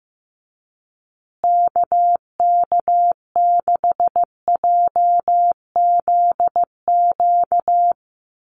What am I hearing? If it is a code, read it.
Morse code, heard as a keyed tone, about 15 wpm, 700 Hz. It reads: KK6JZQ
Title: USA Call Signs Repeated 2x 15wpm